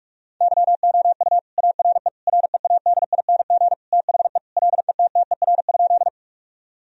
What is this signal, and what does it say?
Morse code, heard as a keyed tone, about 41 wpm, 700 Hz. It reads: YOU ARE READING THE LETTER?
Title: Sentences from Top 1000 Words - Review Part 3 - 45wpm